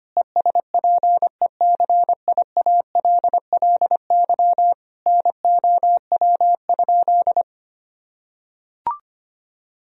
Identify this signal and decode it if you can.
Morse code, heard as a keyed tone, about 25 wpm, 700 Hz. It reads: ESPECIALLY NOW?
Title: Sentences from Top 1000 Words - Review Part 4 - 25wpm